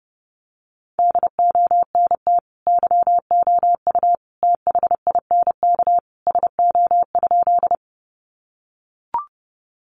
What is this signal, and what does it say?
Morse code, heard as a keyed tone, about 29 wpm, 700 Hz. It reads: DONT YOU THINK SO?
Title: Sentences from Top 300 Words 30wpm